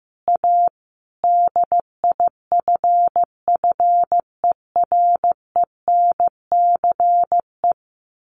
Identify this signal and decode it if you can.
Morse code, heard as a keyed tone, about 15 wpm, 700 Hz. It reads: A DIFFERENCE
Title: Sentences from Top 1400 Words 15wpm